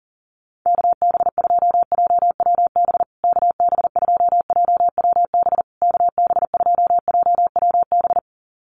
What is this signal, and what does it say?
Morse code, heard as a keyed tone, about 40 wpm, 700 Hz. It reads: KB2JWB KB2JWB KB2JWB
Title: USA Call Signs Repeated 3x 40wpm